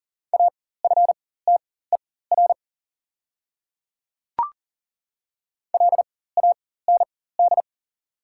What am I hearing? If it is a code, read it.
Morse code, heard as a keyed tone, about 40 wpm, spaced as for 19 wpm, 690 Hz. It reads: AFTER LAND
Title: 4x Character Spacing - Top 300 Words - 40wpm